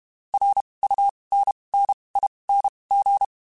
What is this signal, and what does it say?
Morse code, heard as a keyed tone, about 32 wpm, spaced as for 23 wpm, 800 Hz. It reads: RUNNING
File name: (spaced) / QSOs Spaced-mp3s / kc7sac @30 sp